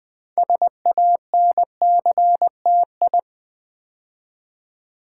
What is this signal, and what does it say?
Morse code, heard as a keyed tone, about 20 wpm, 700 Hz. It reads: SANCTI
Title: Build A Word - 3 Letters At A Time - 8-Letter Words - 20wpm